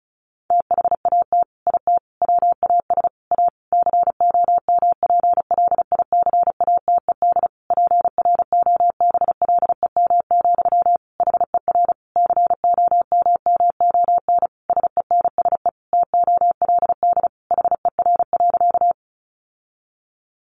Morse code, heard as a keyed tone, about 34 wpm, 700 Hz. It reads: THAT IT WAS A COMPLICATED PROBLEM, HER COMMON SENSE TOLD HER.
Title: Sentences from Top 1200 Words 35wpm